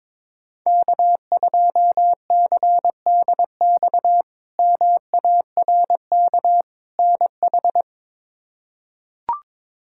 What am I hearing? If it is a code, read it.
Morse code, heard as a keyed tone, about 22 wpm, 700 Hz. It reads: K2CDX MARK NH
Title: K1USN SST QSO Practice - 22wpm